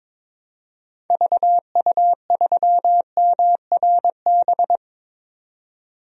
Morse code, heard as a keyed tone, about 22 wpm, 700 Hz. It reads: VU3MRB
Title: DXCC-100 Call Signs - 6 Characters - 22wpm